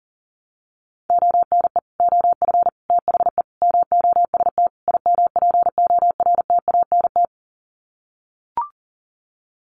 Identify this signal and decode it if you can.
Morse code, heard as a keyed tone, about 40 wpm, 700 Hz. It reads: ONE OF THE MOST IMPORTANT
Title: Sets of 5 Words 40wpm